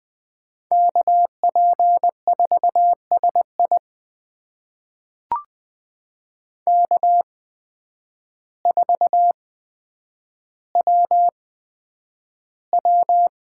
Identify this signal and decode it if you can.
Morse code, heard as a keyed tone, about 20 wpm, spaced as for 9 wpm, 700 Hz. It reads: KP4SI K4WW
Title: USA Call Signs - 8x Character Spacing - 20wpm